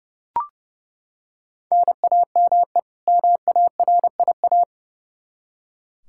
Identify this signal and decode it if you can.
Morse code, heard as a keyed tone, about 32 wpm, 700 Hz. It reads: NAME MARIA
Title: Practice QSO - Exchange 1 - Beginning - 30wpm